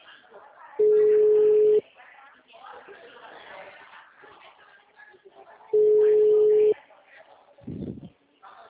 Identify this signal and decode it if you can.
Ringback tone (Europe, 425 Hz), with speech.
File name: out- -2032- - - .58614